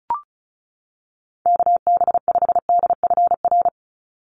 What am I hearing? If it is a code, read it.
Morse code, heard as a keyed tone, about 34 wpm, 700 Hz. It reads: KB5DFR
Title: USA Call Signs 35wpm